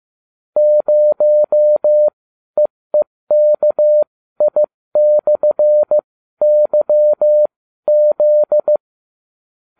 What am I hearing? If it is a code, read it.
Morse code, heard as a keyed tone, about 15 wpm, spaced as for 12 wpm, 600 Hz. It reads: 0EEKI/YZ
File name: cw / ubacw30 3